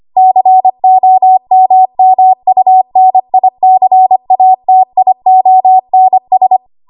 Morse code, heard as a keyed tone, about 25 wpm, 750 Hz. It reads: COMMUNICATIONS
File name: Archive / 25 WPM / 25WPM